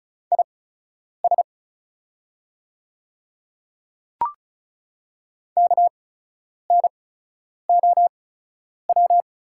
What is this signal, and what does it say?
Morse code, heard as a keyed tone, about 36 wpm, spaced as for 10 wpm, 700 Hz. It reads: IS KNOW
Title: 8x Character Spacing - Top 200 Words - 35wpm